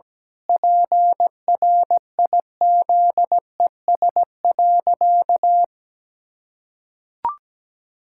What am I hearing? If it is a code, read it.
Morse code, heard as a keyed tone, about 17 wpm, 700 Hz. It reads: PRIZES.
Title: Pangram Challenge v2 - 17wpm